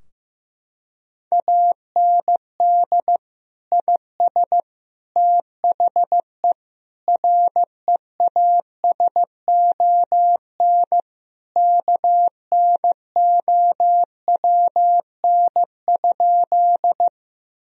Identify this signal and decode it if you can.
Morse code, heard as a keyed tone, about 15 wpm, 700 Hz. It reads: AND IS THE REASON KNOWN?
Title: Sentences from Top 1000 Words - Review Part 3 - 15wpm